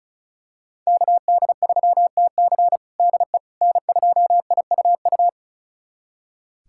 Morse code, heard as a keyed tone, about 34 wpm, 700 Hz. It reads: KD3TC DE N2IUU
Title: USA Callsign DE Callsign - 5 Characters - 35wpm